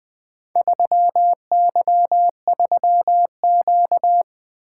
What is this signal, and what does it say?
Morse code, heard as a keyed tone, about 20 wpm, 700 Hz. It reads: 3Y3Q
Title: Four Letter-Number Combinations - 20wpm